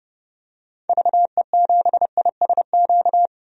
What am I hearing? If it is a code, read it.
Morse code, heard as a keyed tone, about 30 wpm, 700 Hz. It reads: VE7ISQ
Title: DXCC-100 Call Signs - 6 Characters - 30wpm